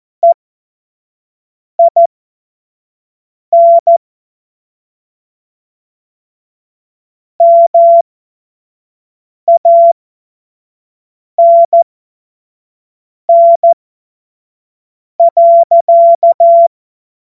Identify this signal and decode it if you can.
Morse code, heard as a keyed tone, about 14 wpm, spaced as for 5 wpm, 680 Hz. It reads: EIN MANN.
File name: Morsekurs / Witze / Witz 25. Januar 2022, 25 F70